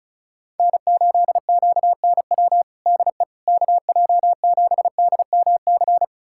Morse code, heard as a keyed tone, about 36 wpm, 700 Hz. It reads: N8QNW DE KJ7DMC
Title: USA Callsign DE Callsign 35wpm